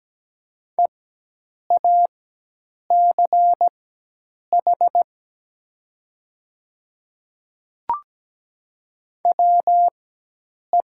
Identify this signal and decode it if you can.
Morse code, heard as a keyed tone, about 17 wpm, spaced as for 8 wpm, 700 Hz. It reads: EACH WE
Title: 4x Character Spacing - Top 100 Words - 17wpm